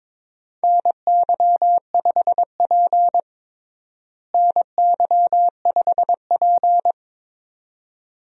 Morse code, heard as a keyed tone, about 22 wpm, 700 Hz. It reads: NY5P NY5P
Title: USA Call Signs - 4 Characters - Repeated 2x - 22wpm